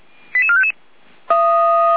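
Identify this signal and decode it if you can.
Five-tone selective call (ZVEI), heard as quick runs of five tones, tones of about 70 ms. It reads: 88455
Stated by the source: ZVEI five-tone selective call 88252 received with an SDR.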